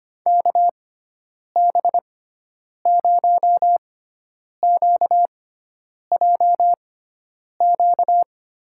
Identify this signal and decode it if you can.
Morse code, heard as a keyed tone, about 25 wpm, spaced as for 9 wpm, 700 Hz. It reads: KB0QJQ
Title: USA Call Signs - 6x Character Spacing - 25wpm